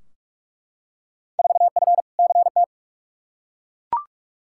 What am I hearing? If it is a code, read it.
Morse code, heard as a keyed tone, about 42 wpm, 700 Hz. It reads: 4F CT
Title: Field Day Exchanges Encore - 45wpm